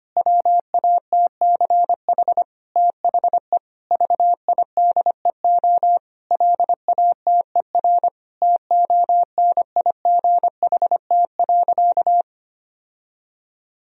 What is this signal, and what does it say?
Morse code, heard as a keyed tone, about 25 wpm, 700 Hz. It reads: WATCH THE VIDEO LATER TONIGHT.
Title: Sentences from Top 1000 Words - Review Part 4 - 25wpm